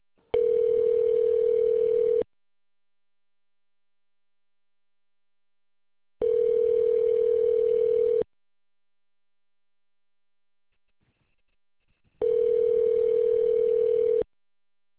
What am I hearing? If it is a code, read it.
Ringback tone (North America), 440 + 480 Hz.